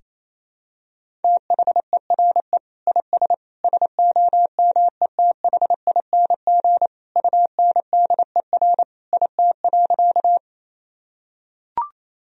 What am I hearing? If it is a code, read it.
Morse code, heard as a keyed tone, about 28 wpm, 700 Hz. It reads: THERE IS SOMETHING UNDER IT.
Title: Sentences from Top 1000 Words - Review Part 1 - 28wpm